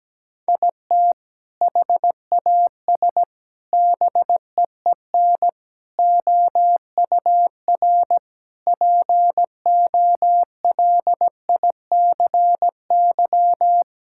Morse code, heard as a keyed tone, about 17 wpm, 700 Hz. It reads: IT HAS BEEN OUR POLICY
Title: Sentences from Top 1000 Words - Review Part 5 - 17wpm